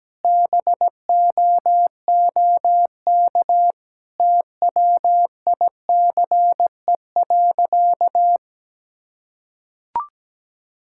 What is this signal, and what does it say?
Morse code, heard as a keyed tone, about 17 wpm, 700 Hz. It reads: BOOK TWICE.
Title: Sentences from Top 1600 Words 17wpm